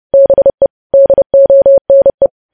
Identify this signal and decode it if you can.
Morse code, heard as a keyed tone, about 30 wpm, 550 Hz. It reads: BE DONE
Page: news headlines in morse code at 30 wpm